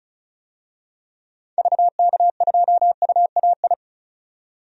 Morse code, heard as a keyed tone, about 36 wpm, 700 Hz. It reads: VK2UAI
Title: DXCC-100 Call Signs - 6 Characters - 35wpm